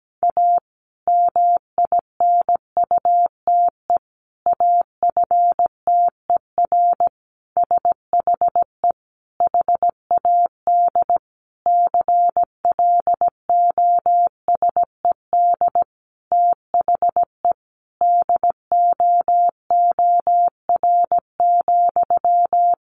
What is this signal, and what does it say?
Morse code, heard as a keyed tone, about 17 wpm, 700 Hz. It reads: A MINUTE AFTER SHE HAD CLOSED THE DOOR,
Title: Sentences from Top 1400 Words 17wpm